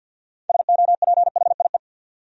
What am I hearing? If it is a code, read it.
Morse code, heard as a keyed tone, about 49 wpm, 700 Hz. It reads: SOPHIE
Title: USA Common Names - 50wpm